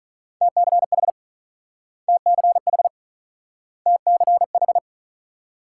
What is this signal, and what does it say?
Morse code, heard as a keyed tone, about 43 wpm, 700 Hz. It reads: TCH TCH TCH
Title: Speed-Racing Multi-letter Phonemes - 35wpm